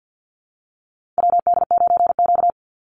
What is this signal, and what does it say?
Morse code, heard as a keyed tone, about 50 wpm, 700 Hz. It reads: WD9Q
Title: Call Signs - 50wpm